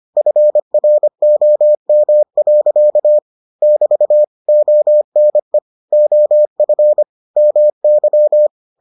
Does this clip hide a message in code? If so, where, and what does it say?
Yes: Morse code from 0.2 to 8.5 s, heard as a keyed tone, about 25 wpm, 600 Hz. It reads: FROM. <BT> ONE OF MY